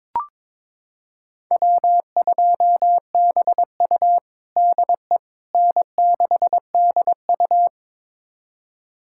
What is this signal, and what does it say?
Morse code, heard as a keyed tone, about 22 wpm, 700 Hz. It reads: W2BU DE N6DU
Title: USA Callsign DE Callsign - 4 Characters - 22wpm